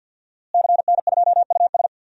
Morse code, heard as a keyed tone, about 50 wpm, 700 Hz. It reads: CN2US
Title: DXCC-100 Call Signs - 5 Characters - Repeated 2x - 50wpm